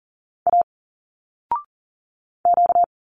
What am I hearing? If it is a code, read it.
Morse code, heard as a keyed tone, about 40 wpm, 700 Hz. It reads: A Q
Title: Single Letters 40wpm